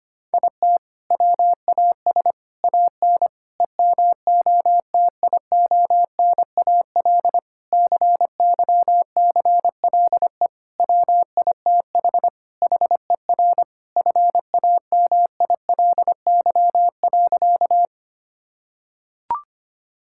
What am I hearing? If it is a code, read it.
Morse code, heard as a keyed tone, about 25 wpm, 700 Hz. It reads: IT WAS AN EMOTIONAL CYCLE WITH HER FAMILY.
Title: Sentences from Top 1000 Words - Review Part 6 - 25wpm